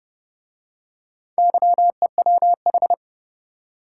Morse code, heard as a keyed tone, about 29 wpm, 700 Hz. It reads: YEWH